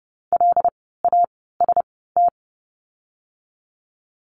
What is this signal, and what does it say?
Morse code, heard as a keyed tone, about 30 wpm, spaced as for 17 wpm, 700 Hz. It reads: LAST
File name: Lesson 17f - Letter u - Words 30wpm